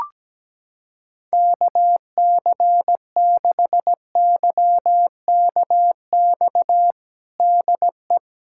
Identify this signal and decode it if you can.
Morse code, heard as a keyed tone, about 17 wpm, 700 Hz. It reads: KC6YKX DE
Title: USA Callsign DE Callsign - 6 Characters - 17wpm